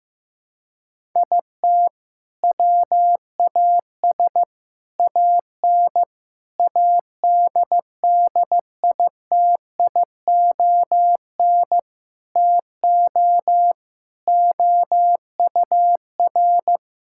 Morse code, heard as a keyed tone, about 15 wpm, 700 Hz. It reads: IT WAS AN ADDITION TO OUR
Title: Sentences from Top 1000 Words - Review Part 6 - 15wpm